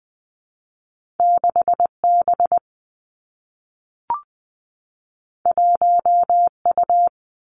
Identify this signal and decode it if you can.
Morse code, heard as a keyed tone, about 20 wpm, 700 Hz. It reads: 6 B 1 U
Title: Two Challenging Letter-Number Combinations 20wpm